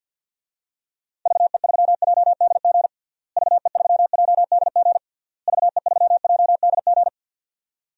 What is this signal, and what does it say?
Morse code, heard as a keyed tone, about 52 wpm, 700 Hz. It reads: VE3JDG VE3JDG VE3JDG
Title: DXCC-100 Call Signs - 6 Characters - Repeated 3x - 50wpm